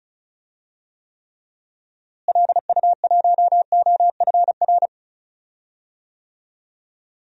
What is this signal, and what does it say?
Morse code, heard as a keyed tone, about 36 wpm, 700 Hz. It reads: LU1OFR
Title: DXCC-100 Call Signs - 6 Characters - Repeated 3x - 35wpm